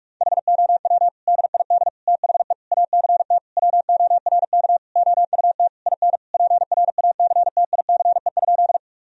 Morse code, heard as a keyed tone, about 48 wpm, 700 Hz. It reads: HOW DID THE ACT WORK OUT IN PRACTICE?
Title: Sentences from Top 1000 Words - Review Part 4 - 45wpm